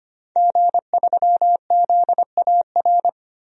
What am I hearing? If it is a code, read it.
Morse code, heard as a keyed tone, about 25 wpm, 700 Hz. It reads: G3ZAR
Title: DXCC-100 Call Signs - 5 Characters - 25wpm